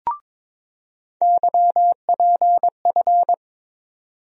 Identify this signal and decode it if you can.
Morse code, heard as a keyed tone, about 22 wpm, 700 Hz. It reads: YPF
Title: USA Call Sign Postfixes - 3 Letters 22wpm